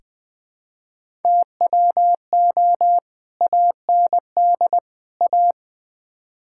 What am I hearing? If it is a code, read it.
Morse code, heard as a keyed tone, about 20 wpm, 700 Hz. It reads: TWO AND A
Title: Build A Phrase - 4-Word Phrases - 20wpm